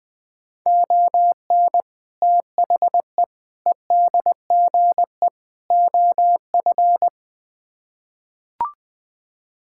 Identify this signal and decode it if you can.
Morse code, heard as a keyed tone, about 21 wpm, 700 Hz. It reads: ON THE EDGE OF
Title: Build A Phrase - 4-Word Phrases - 20wpm